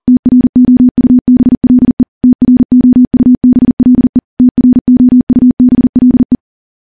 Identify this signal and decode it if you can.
Morse code, heard as a keyed tone, about 40 wpm, 250 Hz. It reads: TROUBLE TROUBLE TROUBLE
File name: morse / morse-training-40WPM